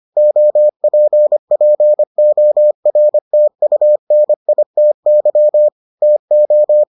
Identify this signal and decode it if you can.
Morse code, heard as a keyed tone, about 25 wpm, 600 Hz. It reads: OPPORTUNITY TO